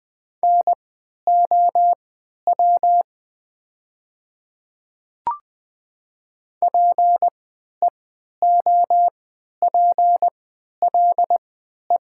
Morse code, heard as a keyed tone, about 20 wpm, spaced as for 11 wpm, 700 Hz. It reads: NOW PEOPLE
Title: 3x Character Spacing - Top 300 Words - 20wpm